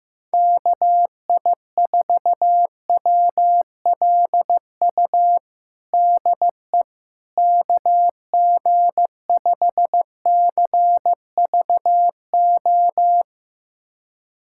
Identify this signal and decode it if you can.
Morse code, heard as a keyed tone, about 15 wpm, 700 Hz. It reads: KI4WLU DE KG5CVO
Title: USA Callsign DE Callsign - 6 Characters - 15wpm